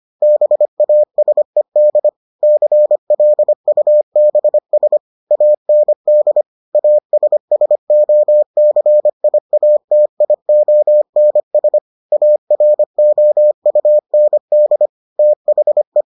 Morse code, heard as a keyed tone, about 25 wpm, 600 Hz. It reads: BASED CLUBS AND ASSOCIATIONS AROUND THE